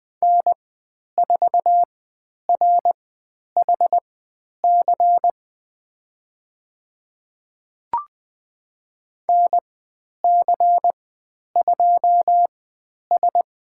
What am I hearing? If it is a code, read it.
Morse code, heard as a keyed tone, about 20 wpm, spaced as for 10 wpm, 700 Hz. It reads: N4RHC NC2S
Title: 102 Lesson 17g - Letter u - Call Signs 20wpm